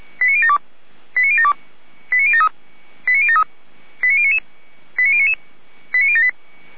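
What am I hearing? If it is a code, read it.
Five-tone selective call (ZVEI), heard as quick runs of five tones, tones of about 70 ms. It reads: 89072 89072 89073 89073 89099 89099 89078
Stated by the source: ZVEI five-tone selective call received on an SDR.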